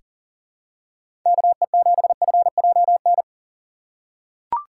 Morse code, heard as a keyed tone, about 40 wpm, 700 Hz. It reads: KE7FJN
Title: USA Call Signs - 6 Characters - 3x Character Spacing - 40wpm